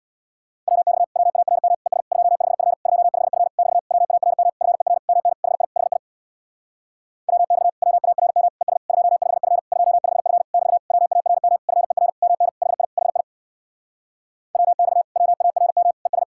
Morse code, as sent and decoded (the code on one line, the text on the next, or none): .-- -..- / .-- .- .-. -- / . ... / ..--- ..... ..-. / ..--- ..... ..-. / -...- / .-- .- .-. -- / ..-. .-. / -- -. / .... .. / .... .. / .-- -..- / .-- .- .-. -- / . ... / ..--- ..... ..-. / ..--- ..... ..-. / -...- / .-- .- .-. -- / ..-. .-. / -- -. / .... .. / .... .. / .-- -..- / .-- .- .-. -- / . ...
WX WARM ES 25F 25F <BT> WARM FER MN HI HI WX WARM ES 25F 25F <BT> WARM FER MN HI HI WX WARM ES